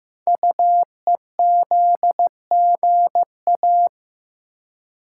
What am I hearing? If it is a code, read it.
Morse code, heard as a keyed tone, about 15 wpm, 700 Hz. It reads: UEZGA
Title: Code Groups - 5 Letter:Numbers - 15wpm